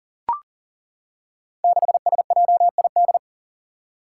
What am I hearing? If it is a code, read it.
Morse code, heard as a keyed tone, about 39 wpm, 700 Hz. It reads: 6SJID
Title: Five Challenging Letter-Number Combinations 40wpm